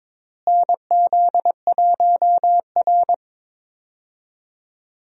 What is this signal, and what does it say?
Morse code, heard as a keyed tone, about 21 wpm, 700 Hz. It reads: NZ1R
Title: USA Call Signs - 4 Characters - Repeated 3x - 22wpm